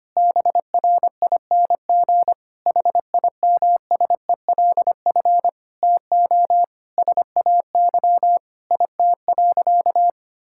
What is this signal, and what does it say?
Morse code, heard as a keyed tone, about 25 wpm, 700 Hz. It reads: BRING HIMSELF TO SAY IT.
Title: Sentences from Top 1600 Words 25wpm